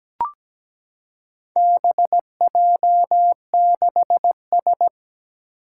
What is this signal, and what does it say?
Morse code, heard as a keyed tone, about 17 wpm, 700 Hz. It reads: BJ6S
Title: Four Challenging Letter-Number Combinations 17wpm